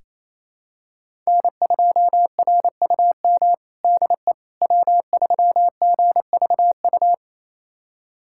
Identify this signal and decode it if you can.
Morse code, heard as a keyed tone, about 28 wpm, 700 Hz. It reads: N2RUM DE W3GVU
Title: USA Callsign DE Callsign - 5 Characters - 28wpm